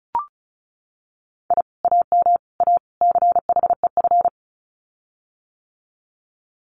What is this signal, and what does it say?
Morse code, heard as a keyed tone, about 34 wpm, 700 Hz. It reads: I AM A CHEF
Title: Common QSO Phrases Encore 35wpm